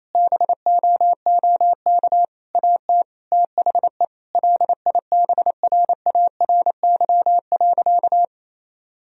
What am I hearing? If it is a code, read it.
Morse code, heard as a keyed tone, about 28 wpm, 700 Hz. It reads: BOOK AT THE LIBRARY.
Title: Sentences from Top 1000 Words - Review Part 1 - 28wpm